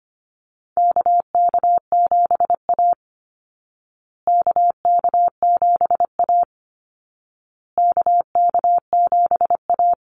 Morse code, heard as a keyed tone, about 25 wpm, 700 Hz. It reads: KK7A KK7A KK7A
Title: Call Signs Repeated 2x - 25wpm